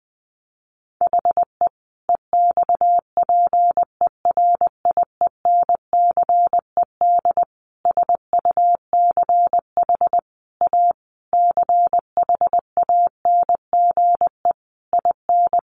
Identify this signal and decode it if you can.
Morse code, heard as a keyed tone, about 20 wpm, 700 Hz. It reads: HE EXPERIENCED SUCH A CHANGE IN
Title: Sentences from Top 1100 Words 20wpm